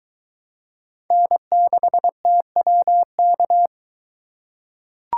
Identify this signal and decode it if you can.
Morse code, heard as a keyed tone, about 23 wpm, 700 Hz. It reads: N6TWK